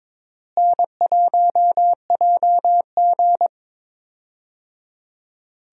Morse code, heard as a keyed tone, about 22 wpm, 700 Hz. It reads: N1JG
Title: USA Call Signs - 4 Characters - Repeated 3x - 22wpm